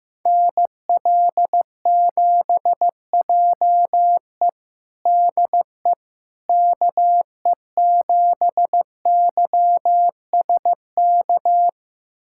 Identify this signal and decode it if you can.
Morse code, heard as a keyed tone, about 15 wpm, 700 Hz. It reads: NL7JE DE KE7YSK
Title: USA Callsign DE Callsign 15wpm